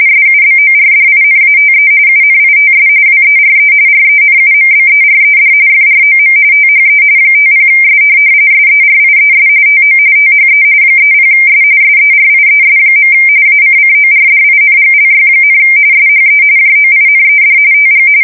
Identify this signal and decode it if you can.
RTTY, about 45.45 baud. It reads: SUBORDINATE UNITS WERE RENUMBERED FROM THE 50S SERIES TO THE 30S SERIES. ACCORDINGLY, SHE JOINED TG